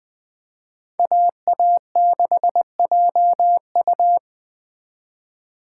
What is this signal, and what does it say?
Morse code, heard as a keyed tone, about 20 wpm, 700 Hz. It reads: AA6JU
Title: Speed Racing - USA Call Signs 17wpm